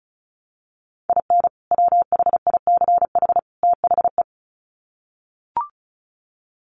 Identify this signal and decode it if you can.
Morse code, heard as a keyed tone, about 34 wpm, 700 Hz. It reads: IN WHICH THE
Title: Sets of 3 Words 35wpm